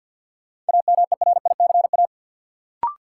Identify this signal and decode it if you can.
Morse code, heard as a keyed tone, about 53 wpm, 700 Hz. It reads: AMERICA
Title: Sets of 5 Words - New Word Drill - 50wpm